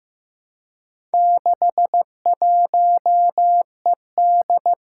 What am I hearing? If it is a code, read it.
Morse code, heard as a keyed tone, about 15 wpm, 700 Hz. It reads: 61ED
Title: Four Challenging Letter-Number Combinations 15wpm